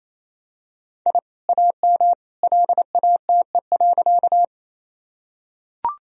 Morse code, heard as a keyed tone, about 28 wpm, 700 Hz. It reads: I AM LATE.
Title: Sentences from Top 1000 Words - Review Part 2 - 28wpm